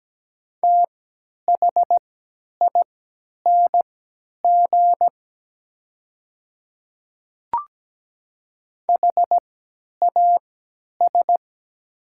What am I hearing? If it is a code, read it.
Morse code, heard as a keyed tone, about 17 wpm, spaced as for 10 wpm, 700 Hz. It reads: THING HAS
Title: 3x Character Spacing - Top 100 Words - 17wpm